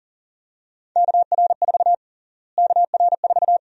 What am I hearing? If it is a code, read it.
Morse code, heard as a keyed tone, about 37 wpm, 700 Hz. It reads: KR4 KR4
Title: USA Call Sign Prefixes - Repeated 2x - 40wpm